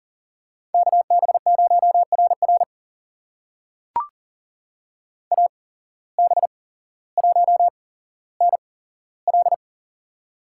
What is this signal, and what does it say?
Morse code, heard as a keyed tone, about 40 wpm, spaced as for 17 wpm, 700 Hz. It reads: KB0RR AB1NL
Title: USA Call Signs - 5 Characters - 8x Character Spacing - 40wpm